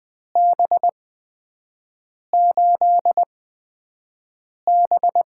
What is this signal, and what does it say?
Morse code, heard as a keyed tone, about 20 wpm, spaced as for 5 wpm, 700 Hz. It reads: B8B
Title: USA Call Signs - 8x Character Spacing - 20wpm